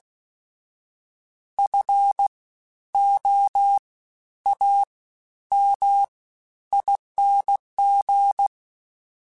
Morse code, heard as a keyed tone, about 16 wpm, 800 Hz. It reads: F O A M ING
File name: Special INGs-mp3s / Special INGs-mp3s / Fast ing @15